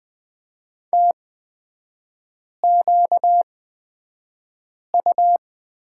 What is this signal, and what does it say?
Morse code, heard as a keyed tone, about 20 wpm, spaced as for 5 wpm, 700 Hz. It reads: TQU
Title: 236 Lesson 38e - Letter q - Three Characters 20wpm